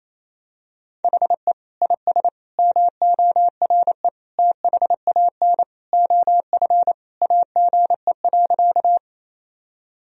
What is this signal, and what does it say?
Morse code, heard as a keyed tone, about 27 wpm, 700 Hz. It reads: HE IS MORE THAN OF AGE.
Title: Sentences from Top 1000 Words - Review Part 3 - 28wpm